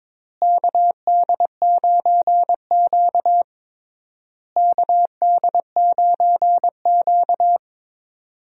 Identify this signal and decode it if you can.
Morse code, heard as a keyed tone, about 22 wpm, 700 Hz. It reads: KD9Q KD9Q
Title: USA Call Signs - 4 Characters - Repeated 3x - 22wpm